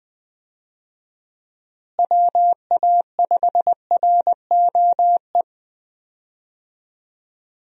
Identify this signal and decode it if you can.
Morse code, heard as a keyed tone, about 21 wpm, 700 Hz. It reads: WA5ROE
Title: USA Call Signs - 6 Characters - Repeated 3x - 20wpm